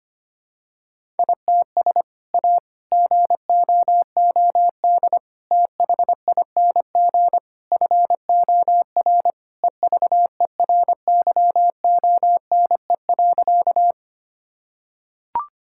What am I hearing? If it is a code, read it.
Morse code, heard as a keyed tone, about 25 wpm, 700 Hz. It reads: ITS A GOOD THING FOR EVERYONE.
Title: Sentences from Top 1000 Words - Review Part 5 - 25wpm